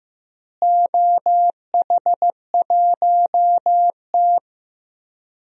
Morse code, heard as a keyed tone, about 15 wpm, 700 Hz. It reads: OH1T
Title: DXCC-100 Call Signs - 4 Characters - Repeated 3x - 15wpm